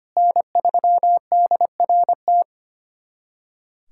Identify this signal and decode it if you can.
Morse code, heard as a keyed tone, about 25 wpm, 700 Hz. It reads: N3DRT
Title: USA Callsign DE Callsign - 5 Characters - 25wpm